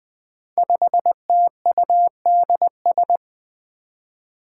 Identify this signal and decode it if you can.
Morse code, heard as a keyed tone, about 20 wpm, 700 Hz. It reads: 5TUDS
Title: Five Challenging Letter-Number Combinations 20wpm